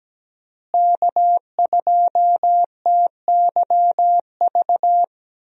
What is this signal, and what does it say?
Morse code, heard as a keyed tone, about 17 wpm, 700 Hz. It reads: K2TYV
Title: USA Call Signs - 5 Characters - Repeated 2x - 17wpm